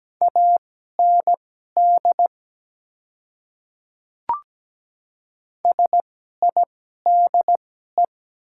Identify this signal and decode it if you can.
Morse code, heard as a keyed tone, about 17 wpm, spaced as for 12 wpm, 700 Hz. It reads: AND SIDE
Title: 2x Character Spacing - Top 100 Words - 17wpm